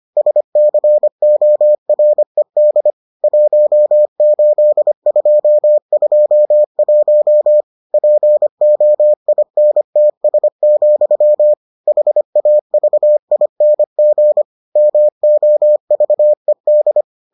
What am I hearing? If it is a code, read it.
Morse code, heard as a keyed tone, about 25 wpm, 600 Hz. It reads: SCORED 18221 POINTS, HAVING MOVED